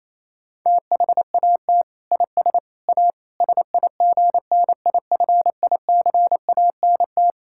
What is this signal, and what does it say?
Morse code, heard as a keyed tone, about 28 wpm, 700 Hz. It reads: THAT IS A SIGNIFICANT
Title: Sentences from Top 1000 Words - Review Part 6 - 28wpm